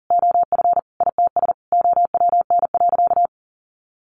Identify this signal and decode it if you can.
Morse code, heard as a keyed tone, about 40 wpm, 700 Hz. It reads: OF ITS OWN.
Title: Sentences from Top 1200 Words 40wpm